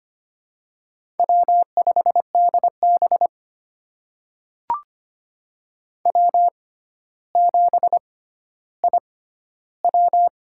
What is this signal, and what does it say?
Morse code, heard as a keyed tone, about 25 wpm, spaced as for 13 wpm, 700 Hz. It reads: W5DB W7IW
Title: USA Call Signs - 4 Characters - 6x Character Spacing - 25wpm